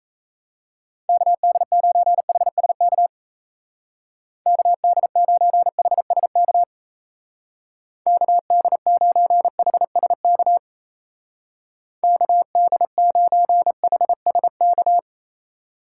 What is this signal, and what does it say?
Morse code, heard as a keyed tone, about 35 wpm, 700 Hz. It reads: KD9HSK KD9HSK KD9HSK KD9HSK
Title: Speed Racing - USA 2x3 Call Signs 28wpm